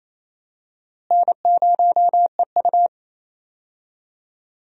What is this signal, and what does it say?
Morse code, heard as a keyed tone, about 28 wpm, 700 Hz. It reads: N0EU
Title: USA Call Signs - 4 Characters - Repeated 3x - 28wpm